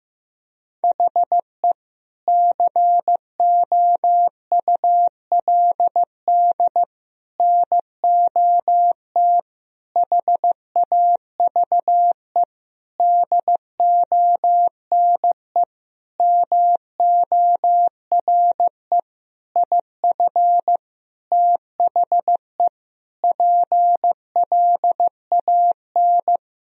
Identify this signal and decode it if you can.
Morse code, heard as a keyed tone, about 15 wpm, 700 Hz. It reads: HE COULD NOT HAVE DONE MORE IF THE PLAN
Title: Sentences from Top 1000 Words - Review Part 2 - 15wpm